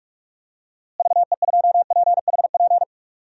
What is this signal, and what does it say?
Morse code, heard as a keyed tone, about 41 wpm, 700 Hz. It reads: VE2PHP
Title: DXCC-100 Call Signs - 6 Characters - Repeated 2x - 45wpm